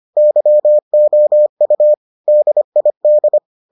Morse code, heard as a keyed tone, about 25 wpm, 600 Hz. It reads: YOU DID